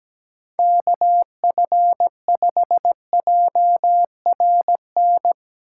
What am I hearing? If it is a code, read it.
Morse code, heard as a keyed tone, about 19 wpm, 700 Hz. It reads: KF5JRN
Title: Speed Racing - USA 2x3 Call Signs 17wpm